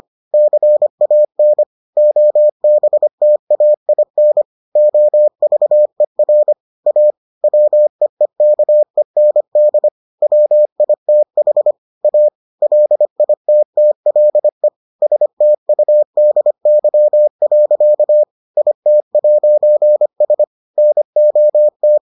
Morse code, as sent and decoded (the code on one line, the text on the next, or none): -.-. .- -. / --- -... - .- .. -. / --- ...- . .-. / .- / .-- . -.- . -. -.. / .-- .. - .... / .- / .-.. .. - - .-.. / ... - ..- -.. -.-- .-.-.- / .. - .----. ... / -. --- -
CAN OBTAIN OVER A WEEKEND WITH A LITTLE STUDY. IT'S NOT